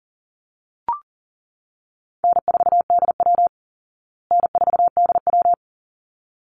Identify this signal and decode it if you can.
Morse code, heard as a keyed tone, about 40 wpm, 700 Hz. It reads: N4DW N4DW
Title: Call Signs Repeated 3x - 40wpm